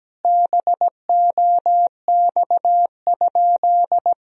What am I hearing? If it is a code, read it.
Morse code, heard as a keyed tone, about 17 wpm, 700 Hz. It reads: BOX?
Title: Sentences from Top 1000 Words - Review Part 2 - 17wpm